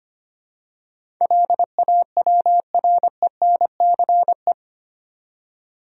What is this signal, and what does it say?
Morse code, heard as a keyed tone, about 25 wpm, 700 Hz. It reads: LAWRENCE
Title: USA Common Names - 25wpm